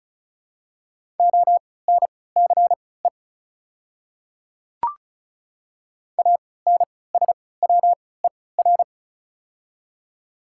Morse code, heard as a keyed tone, about 36 wpm, spaced as for 20 wpm, 690 Hz. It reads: ONCE ANSWER
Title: 3x Character Spacing - Top 300 Words - 35wpm